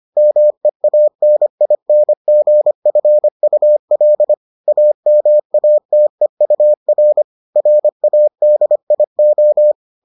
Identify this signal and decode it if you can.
Morse code, heard as a keyed tone, about 25 wpm, 600 Hz. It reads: MEANINGFUL AMATEUR RADIO